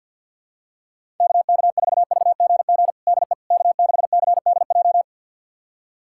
Morse code, heard as a keyed tone, about 48 wpm, 700 Hz. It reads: KK4VGG DE K6CDJ
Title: USA Callsign DE Callsign 50wpm